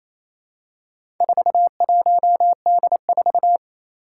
Morse code, heard as a keyed tone, about 28 wpm, 700 Hz. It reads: 41D4
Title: Four Challenging Letter-Number Combinations 28wpm